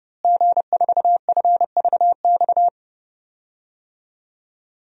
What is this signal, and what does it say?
Morse code, heard as a keyed tone, about 30 wpm, 700 Hz. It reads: G4FVX
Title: DXCC-100 Call Signs - 5 Characters - Repeated 2x - 30wpm